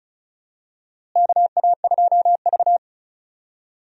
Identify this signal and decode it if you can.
Morse code, heard as a keyed tone, about 36 wpm, 700 Hz. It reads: KA2V
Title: USA Call Signs - 4 Characters - Repeated 3x - 35wpm